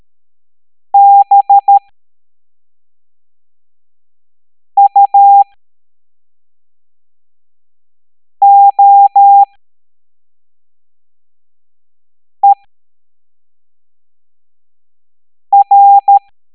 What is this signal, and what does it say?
Morse code, heard as a keyed tone, about 13 wpm, spaced as for 3 wpm, 800 Hz. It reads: BUOER